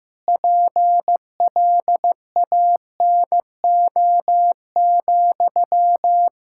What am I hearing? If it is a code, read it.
Morse code, heard as a keyed tone, about 15 wpm, 700 Hz. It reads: PLANO,
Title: Practice QSO - Exchange 1 - Beginning - 15wpm